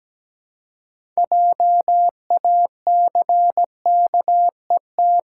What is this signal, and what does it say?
Morse code, heard as a keyed tone, about 17 wpm, 700 Hz. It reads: JACKET
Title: Six Letter Words - 17wpm